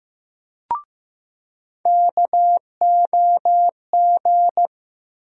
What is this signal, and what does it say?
Morse code, heard as a keyed tone, about 15 wpm, 700 Hz. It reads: KOG
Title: USA Call Sign Postfixes - 3 Letters 15wpm